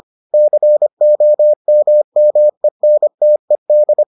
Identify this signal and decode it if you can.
Morse code, heard as a keyed tone, about 25 wpm, 600 Hz. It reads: COMMENTED